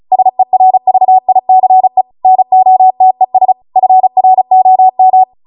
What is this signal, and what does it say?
Morse code, heard as a keyed tone, about 36 wpm, 750 Hz. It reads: SERVICE NOTES FROM